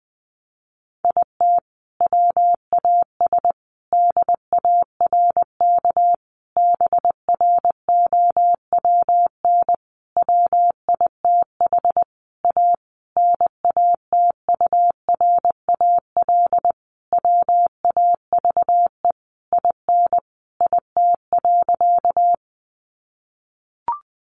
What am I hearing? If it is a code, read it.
Morse code, heard as a keyed tone, about 20 wpm, 700 Hz. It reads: IT WAS DARK BROWN WITH A NATURAL WAVE IN IT.
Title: Sentences from Top 1400 Words 20wpm